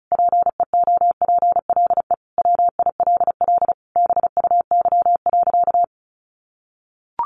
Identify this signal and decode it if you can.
Morse code, heard as a keyed tone, about 34 wpm, 700 Hz. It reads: PEOPLE WILL BUY.
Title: Sentences from Top 1300 Words 35wpm